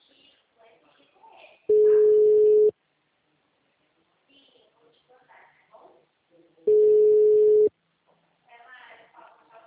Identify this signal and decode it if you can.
Ringback tone (Europe, 425 Hz), with speech.
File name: out- -2016- - - .15536